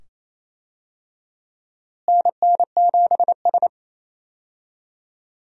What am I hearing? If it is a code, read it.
Morse code, heard as a keyed tone, about 28 wpm, 700 Hz. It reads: NN7S
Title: USA Call Signs - 4 Characters - Repeated 2x - 28wpm